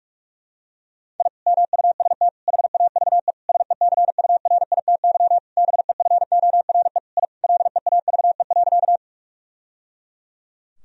Morse code, heard as a keyed tone, about 41 wpm, 700 Hz. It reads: I MUST HAVE SECURITY BEFORE I LEAVE.
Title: Sentences from Top 1000 Words - Review Part 4 - 45wpm